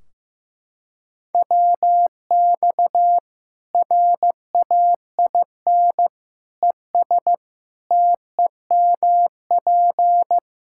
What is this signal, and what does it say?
Morse code, heard as a keyed tone, about 15 wpm, 700 Hz. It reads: WX RAIN ES TEMP
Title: Practice QSO - Exchange 2 - Info - 15wpm